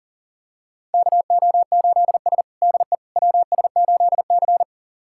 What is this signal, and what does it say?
Morse code, heard as a keyed tone, about 40 wpm, 700 Hz. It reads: KO8S DE WS8C
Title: USA Callsign DE Callsign - 4 Characters - 40wpm